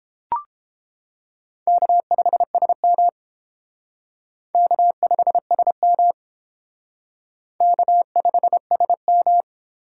Morse code, heard as a keyed tone, about 29 wpm, 700 Hz. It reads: K5SM K5SM K5SM